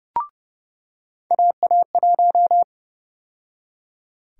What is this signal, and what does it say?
Morse code, heard as a keyed tone, about 30 wpm, 700 Hz. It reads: AA1
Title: USA Call Sign Prefixes - 30wpm